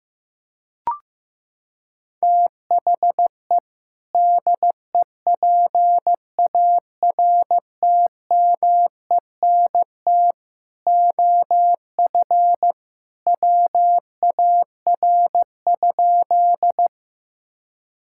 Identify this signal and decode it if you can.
Morse code, heard as a keyed tone, about 15 wpm, 700 Hz. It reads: THE DEPARTMENT OF WAR?
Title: Sentences from Top 1000 Words - Review Part 5 - 15wpm